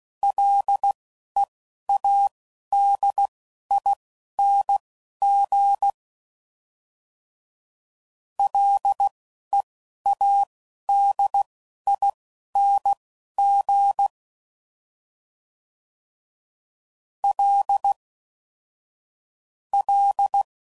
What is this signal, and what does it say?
Morse code, heard as a keyed tone, about 16 wpm, spaced as for 11 wpm, 800 Hz. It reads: LEADING LEADING L L